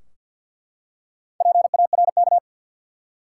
Morse code, heard as a keyed tone, about 50 wpm, 700 Hz. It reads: PARK
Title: Four Letter Words - 50wpm